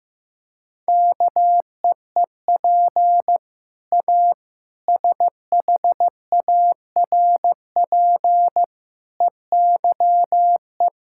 Morse code, heard as a keyed tone, about 16 wpm, 700 Hz. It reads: KEEP A SHARP EYE
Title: Sentences from Top 1600 Words 15wpm